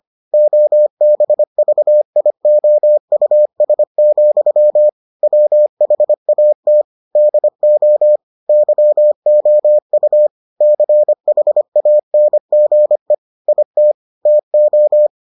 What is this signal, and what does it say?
Morse code, heard as a keyed tone, about 25 wpm, 600 Hz. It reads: OBVIOUS, WHAT DO YOU CHANGE IT TO